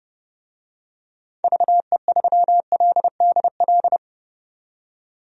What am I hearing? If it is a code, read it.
Morse code, heard as a keyed tone, about 31 wpm, 690 Hz. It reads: VE3LDL
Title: DXCC-100 Call Signs - 6 Characters - 30wpm